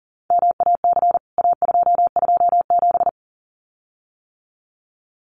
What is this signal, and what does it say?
Morse code, heard as a keyed tone, about 40 wpm, 700 Hz. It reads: MAC A227
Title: CWT Exchange 40wpm